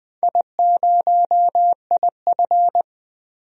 Morse code, heard as a keyed tone, about 20 wpm, 700 Hz. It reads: I0IF
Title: DXCC-100 Call Signs - 4 Characters - 20wpm